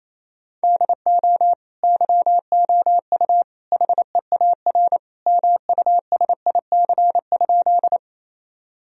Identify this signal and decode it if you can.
Morse code, heard as a keyed tone, about 27 wpm, 700 Hz. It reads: DO YOU HEAR MUSIC?
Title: Sentences from Top 1000 Words - Review Part 1 - 28wpm